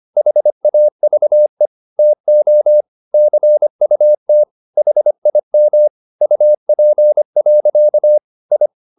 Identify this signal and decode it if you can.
Morse code, heard as a keyed tone, about 25 wpm, 600 Hz. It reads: HAVE TO CUT HIM UP. I
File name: the old man and the sea part 1 25wpm